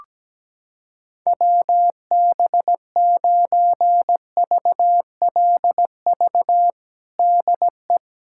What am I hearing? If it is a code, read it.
Morse code, heard as a keyed tone, about 17 wpm, 700 Hz. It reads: WB9VLV DE
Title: USA Callsign DE Callsign 17wpm